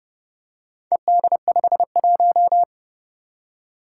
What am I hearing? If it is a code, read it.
Morse code, heard as a keyed tone, about 30 wpm, 700 Hz. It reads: ED51
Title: Four Challenging Letter-Number Combinations 30wpm